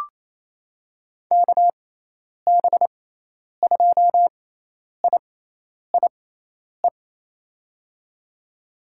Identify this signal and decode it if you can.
Morse code, heard as a keyed tone, about 28 wpm, spaced as for 10 wpm, 700 Hz. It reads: KB2IIE
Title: USA Call Signs - 6 Characters - 6x Character Spacing - 28wpm